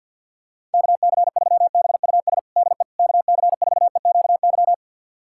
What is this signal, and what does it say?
Morse code, heard as a keyed tone, about 50 wpm, 700 Hz. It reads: KC3BUS DE KC4EQY
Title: USA Callsign DE Callsign 50wpm